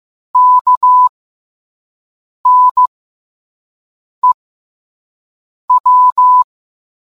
Morse code, heard as a keyed tone, about 15 wpm, spaced as for 5 wpm, 1000 Hz. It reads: KNEW